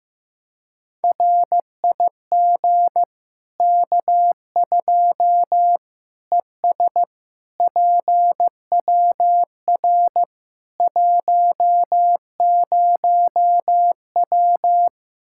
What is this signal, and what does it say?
Morse code, heard as a keyed tone, about 15 wpm, 700 Hz. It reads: RIG K2 ES PWR 10W
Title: Practice QSO - Exchange 2 - Info - 15wpm